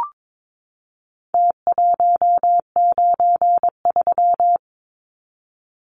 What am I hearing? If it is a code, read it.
Morse code, heard as a keyed tone, about 22 wpm, 700 Hz. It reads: T193
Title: Numbers - Contest Style - 4-Digit 22wpm